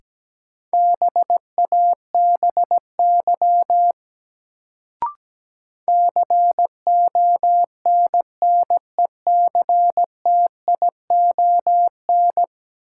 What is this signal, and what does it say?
Morse code, heard as a keyed tone, about 17 wpm, 700 Hz. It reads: BABY CONNECTION
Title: Sentences from Top 1300 Words - New Word Drill Alt - 17wpm